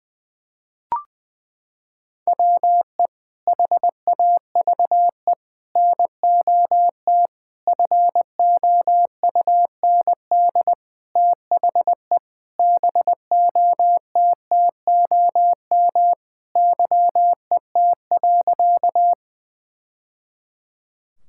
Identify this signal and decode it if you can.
Morse code, heard as a keyed tone, about 20 wpm, 700 Hz. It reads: WE HAVE NOT FOUND THE BOTTOM YET.
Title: Sentences from Top 1000 Words - Review Part 6 - 20wpm